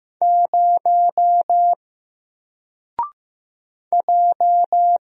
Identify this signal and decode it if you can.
Morse code, heard as a keyed tone, about 15 wpm, 700 Hz. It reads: 0 J
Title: Single Character (Letter:Number:Punctuation:Prosign) Alt 15wpm